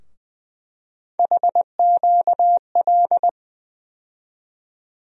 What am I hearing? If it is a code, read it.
Morse code, heard as a keyed tone, about 20 wpm, 700 Hz. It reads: HQL
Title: USA Call Sign Postfixes - 3 Letters 20wpm